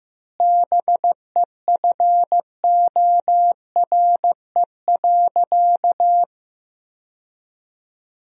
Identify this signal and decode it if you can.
Morse code, heard as a keyed tone, about 15 wpm, 700 Hz. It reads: BEFORE.
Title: 2x Word-Spacing - Sentences from Top 400 Words - 15wpm